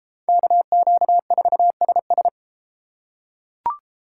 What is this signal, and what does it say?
Morse code, heard as a keyed tone, about 30 wpm, 700 Hz. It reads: KQ4SS